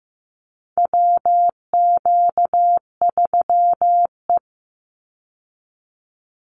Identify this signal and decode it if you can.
Morse code, heard as a keyed tone, about 15 wpm, 700 Hz. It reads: WQ3E
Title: Call Signs Repeated 2x - 15wpm